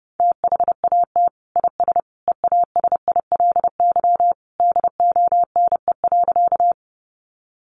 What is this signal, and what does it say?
Morse code, heard as a keyed tone, about 30 wpm, 700 Hz. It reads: THAT IS EASILY DONE.
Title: Sentences from Top 900 Words 30wpm